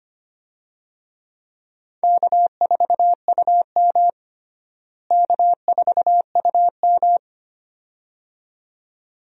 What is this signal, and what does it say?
Morse code, heard as a keyed tone, about 25 wpm, 700 Hz. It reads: K4UM K4UM
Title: USA Call Signs - 4 Characters - Repeated 2x - 25wpm